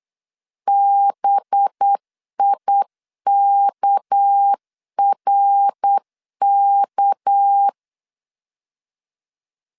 Morse code, heard as a keyed tone, about 8 wpm, 800 Hz. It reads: BIKRK
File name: LEZIONI / 6 PKX0 / 6 PK 8wpm 3003